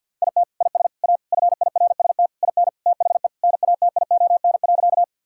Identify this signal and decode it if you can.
Morse code, heard as a keyed tone, about 46 wpm, 700 Hz. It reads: IT IS A FIRST IN THE NATION.
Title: Sentences from Top 1000 Words - Review Part 6 - 50wpm